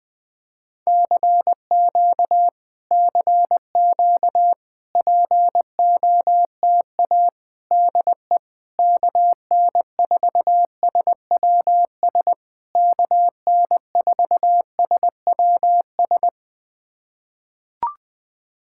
Morse code, heard as a keyed tone, about 20 wpm, 700 Hz. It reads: CQ CQ POTA DE KN4SWS KN4SWS
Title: POTA QSO Practice - 20wpm